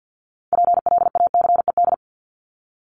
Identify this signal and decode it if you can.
Morse code, heard as a keyed tone, about 50 wpm, 700 Hz. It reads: PLACED
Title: ED Suffix - Top 100 Words with Base Words 50wpm